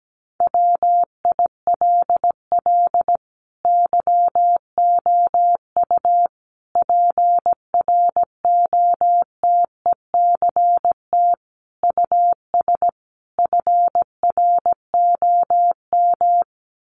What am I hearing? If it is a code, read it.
Morse code, heard as a keyed tone, about 17 wpm, 700 Hz. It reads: WILL YOU PROTECT US FROM
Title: Sentences from Top 900 Words 17wpm